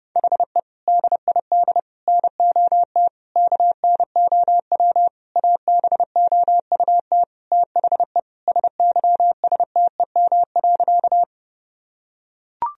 Morse code, heard as a keyed tone, about 32 wpm, 700 Hz. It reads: HE DID NOT KNOW ABOUT THE SYSTEM.
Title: Sentences from Top 1000 Words - Review Part 3 - 30wpm